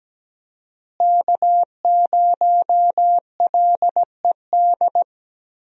Morse code, heard as a keyed tone, about 17 wpm, 700 Hz. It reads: K0LED
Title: USA Call Signs - 5 Characters - Repeated 2x - 17wpm